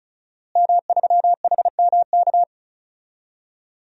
Morse code, heard as a keyed tone, about 36 wpm, 700 Hz. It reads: M3HMK
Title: DXCC-100 Call Signs - 5 Characters - Repeated 2x - 35wpm